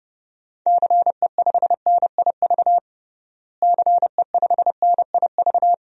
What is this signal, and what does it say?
Morse code, heard as a keyed tone, about 30 wpm, 700 Hz. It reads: CE5NIV CE5NIV
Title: DXCC-100 Call Signs - 6 Characters - Repeated 3x - 30wpm